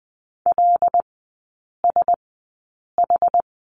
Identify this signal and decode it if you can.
Morse code, heard as a keyed tone, about 20 wpm, spaced as for 8 wpm, 700 Hz. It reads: LSH
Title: 074 Lesson 13e - Letter L - Three Characters 20wpm